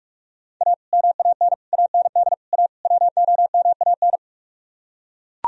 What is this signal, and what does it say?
Morse code, heard as a keyed tone, about 48 wpm, 700 Hz. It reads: A MAN AND A WOMAN